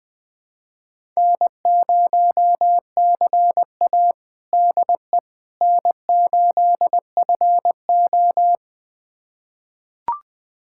Morse code, heard as a keyed tone, about 21 wpm, 700 Hz. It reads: N0CA DE N8FO
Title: USA Callsign DE Callsign - 4 Characters - 20wpm